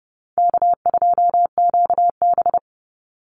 Morse code, heard as a keyed tone, about 30 wpm, 700 Hz. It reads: K2QB
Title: Easy Call Signs - 30wpm